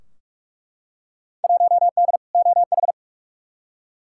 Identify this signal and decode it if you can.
Morse code, heard as a keyed tone, about 46 wpm, 700 Hz. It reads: 1D OH
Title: Field Day Exchanges - 45wpm